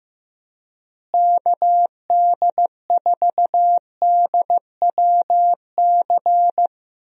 Morse code, heard as a keyed tone, about 15 wpm, 700 Hz. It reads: KD4DWC
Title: USA Call Signs - 6 Characters - Repeated 3x - 15wpm